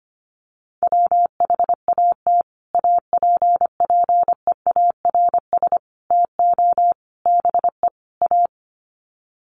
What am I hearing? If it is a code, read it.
Morse code, heard as a keyed tone, about 25 wpm, 700 Hz. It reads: WHAT APPEARS TO BE A
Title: Sets of 5 Words Encore2 25wpm